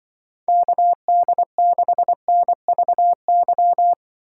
Morse code, heard as a keyed tone, about 25 wpm, 700 Hz. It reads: KD6NVY
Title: Speed Racing - USA 2x3 Call Signs 20wpm